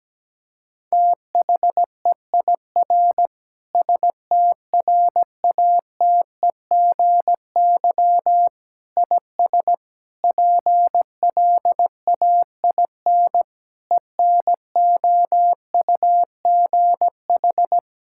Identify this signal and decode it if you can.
Morse code, heard as a keyed tone, about 17 wpm, 700 Hz. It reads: THEIR STRATEGY IS PLAIN ENOUGH
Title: Sentences from Top 1000 Words - Review Part 5 - 17wpm